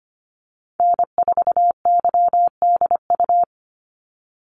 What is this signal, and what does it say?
Morse code, heard as a keyed tone, about 25 wpm, 700 Hz. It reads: N4YDU
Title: Call Signs - 25wpm